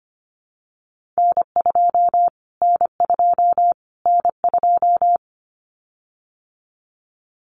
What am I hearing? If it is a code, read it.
Morse code, heard as a keyed tone, about 25 wpm, 700 Hz. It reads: N2 N2 N2
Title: Callsign Prefixes for Top 25 DXCC - Repeated 3x - 25wpm